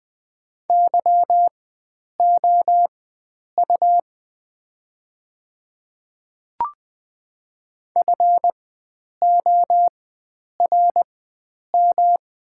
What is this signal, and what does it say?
Morse code, heard as a keyed tone, about 20 wpm, spaced as for 9 wpm, 700 Hz. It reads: YOU FORM
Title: 4x Character Spacing - Top 500 Words - 20wpm